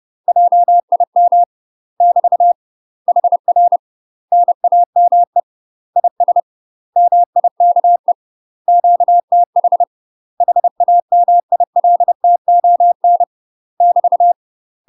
Morse code, heard as a keyed tone, about 30 wpm, 700 Hz. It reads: JIM <BT> HR NAME IS MIKE QTH HAMILTON <BT>
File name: UK MASTER QSOs TRAINING / 30WPM / UK MASTER QSOs 0046